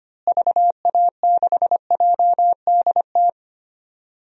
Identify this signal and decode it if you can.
Morse code, heard as a keyed tone, about 25 wpm, 700 Hz. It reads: VA6JDT
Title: DXCC-100 Call Signs - 6 Characters - Repeated 3x - 25wpm